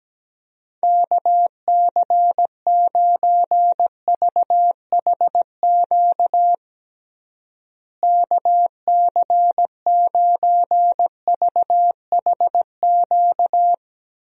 Morse code, heard as a keyed tone, about 17 wpm, 700 Hz. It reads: KC9VHQ KC9VHQ
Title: USA Call Signs - 6 Characters - Repeated 2x - 17wpm